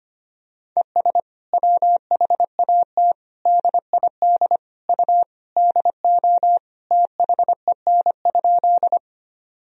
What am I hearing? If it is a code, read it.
Morse code, heard as a keyed tone, about 25 wpm, 700 Hz. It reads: ES WHAT DID U DO THEN?
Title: Sentences from Top 100 Words and Abbreviations 25wpm